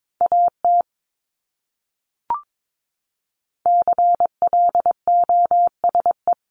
Morse code, heard as a keyed tone, about 22 wpm, 700 Hz. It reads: AT CLOSE
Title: Top 500 Words Repeated 22wpm